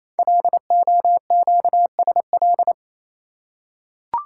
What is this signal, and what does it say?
Morse code, heard as a keyed tone, about 27 wpm, 700 Hz. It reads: LOQSL
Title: Code Groups - 5 Letters - 28wpm